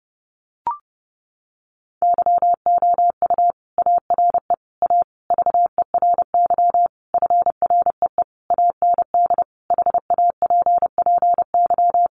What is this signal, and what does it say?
Morse code, heard as a keyed tone, about 30 wpm, 700 Hz. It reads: YOU ARE A VERY FREE AND HAPPY
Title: Sentences from Top 1000 Words 30wpm